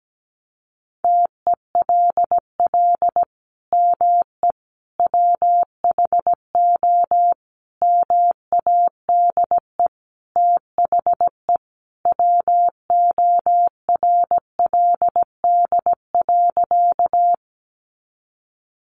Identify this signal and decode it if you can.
Morse code, heard as a keyed tone, about 17 wpm, 700 Hz. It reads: TELL ME WHO MADE THE WORLD.
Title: Sentences from Top 200 Words 17wpm